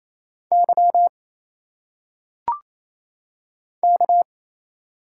Morse code, heard as a keyed tone, about 27 wpm, 700 Hz. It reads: Y K
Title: Single Letters - Phonetic Pronunciation 28wpm